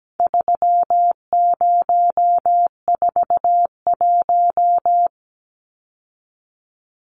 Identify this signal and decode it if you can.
Morse code, heard as a keyed tone, about 17 wpm, 700 Hz. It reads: 3041
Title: Numbers - 4-Digit 17wpm